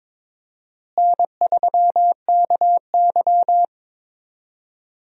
Morse code, heard as a keyed tone, about 21 wpm, 700 Hz. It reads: N3KY
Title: USA Call Signs - 4 Characters - Repeated 2x - 22wpm